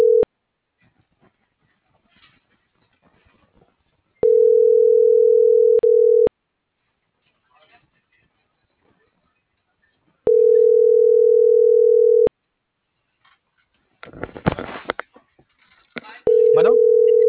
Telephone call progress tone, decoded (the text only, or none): ringback tone